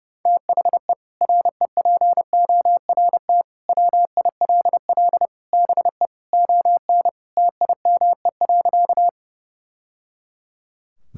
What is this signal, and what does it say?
Morse code, heard as a keyed tone, about 32 wpm, 700 Hz. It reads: THE REPORT WILL BE ON TIME.
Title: Sentences from Top 1000 Words - Review Part 6 - 30wpm